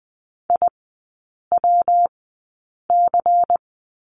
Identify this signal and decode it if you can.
Morse code, heard as a keyed tone, about 20 wpm, spaced as for 8 wpm, 690 Hz. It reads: IWC
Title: 198 Lesson 32e - Letter k - Three Characters 20wpm